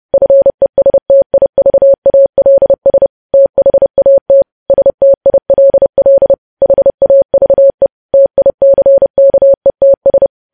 Morse code, heard as a keyed tone, about 30 wpm, 550 Hz. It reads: FESTIVALS THAT STILL HAVE TICKETS
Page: News Headlines in Morse Code at 30 WPM